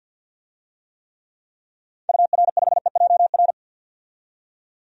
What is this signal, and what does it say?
Morse code, heard as a keyed tone, about 52 wpm, 700 Hz. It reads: UR5EJR
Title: DXCC-100 Call Signs - 6 Characters - 50wpm